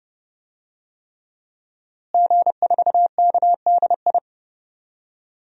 Morse code, heard as a keyed tone, about 31 wpm, 700 Hz. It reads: G4KDI
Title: DXCC-100 Call Signs - 5 Characters - Repeated 3x - 30wpm